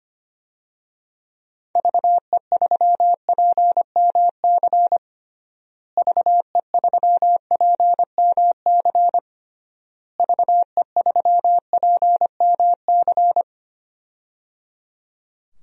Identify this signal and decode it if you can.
Morse code, heard as a keyed tone, about 25 wpm, 700 Hz. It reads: VE3PMC VE3PMC VE3PMC
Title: DXCC-100 Call Signs - 6 Characters - Repeated 3x - 25wpm